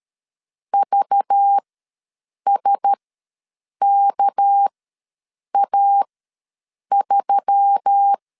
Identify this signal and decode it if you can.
Morse code, heard as a keyed tone, about 13 wpm, spaced as for 7 wpm, 800 Hz. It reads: VSKA3
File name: LEZIONI / 6 PKX0 / 6 PK ws 12wpm 3116